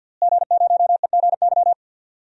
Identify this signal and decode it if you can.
Morse code, heard as a keyed tone, about 46 wpm, 700 Hz. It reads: G0EGY
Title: DXCC-100 Call Signs - 5 Characters - Repeated 3x - 50wpm